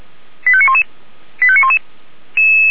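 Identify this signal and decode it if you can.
Five-tone selective call (ZVEI), heard as quick runs of five tones, tones of about 70 ms. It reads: 85911 85911
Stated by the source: ZVEI five-tone selective call 89115 received on SDR.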